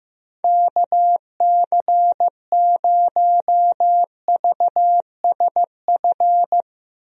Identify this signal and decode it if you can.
Morse code, heard as a keyed tone, about 15 wpm, 700 Hz. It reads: KC0VSF
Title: USA Call Signs - 6 Characters - Repeated 2x - 15wpm